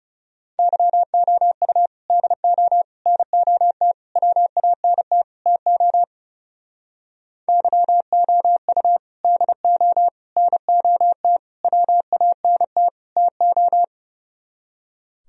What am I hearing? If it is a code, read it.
Morse code, heard as a keyed tone, about 39 wpm, 700 Hz. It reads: YOU DO NOT WANT TO YOU DO NOT WANT TO